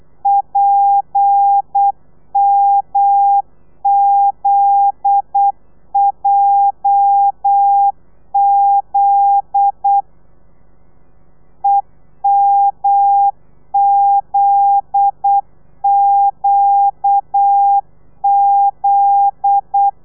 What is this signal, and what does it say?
Morse code, heard as a keyed tone, about 8 wpm, 800 Hz. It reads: PMZJZ EMZQZ
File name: telegrafia / i2tfj / Lez9 35